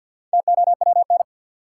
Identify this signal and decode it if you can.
Morse code, heard as a keyed tone, about 53 wpm, 700 Hz. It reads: TOWN
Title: Sentences from Top 500 Words - New Word Drill - 50wpm